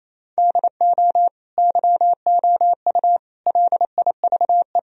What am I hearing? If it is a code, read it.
Morse code, heard as a keyed tone, about 28 wpm, 700 Hz. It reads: DO YOU LIVE
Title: Sentences from Top 1000 Words - Review Part 3 - 28wpm